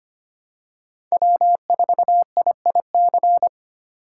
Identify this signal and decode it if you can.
Morse code, heard as a keyed tone, about 25 wpm, 700 Hz. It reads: W4IIC
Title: K1USN SST QSO Practice - 25wpm